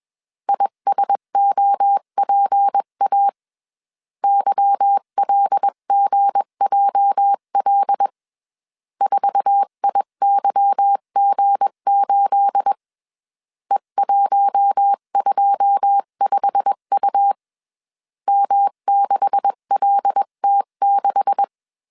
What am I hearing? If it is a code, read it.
Morse code, heard as a keyed tone, about 21 wpm, 800 Hz. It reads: ISOPA YLGJL 4IYG8 E125U M6LT6